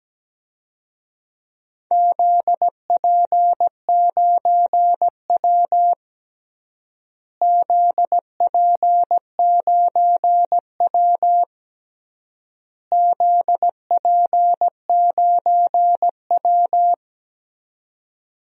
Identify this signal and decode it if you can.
Morse code, heard as a keyed tone, about 17 wpm, 700 Hz. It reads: ZP9W ZP9W ZP9W
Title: DXCC-100 Call Signs - 4 Characters - Repeated 3x - 17wpm